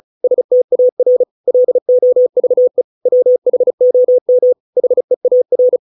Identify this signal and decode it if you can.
Morse code, heard as a keyed tone, about 34 wpm, 500 Hz. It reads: STAR LOVE WHOM HEAR